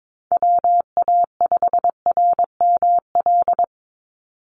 Morse code, heard as a keyed tone, about 22 wpm, 700 Hz. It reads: WA5RML
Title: Call Signs - 22wpm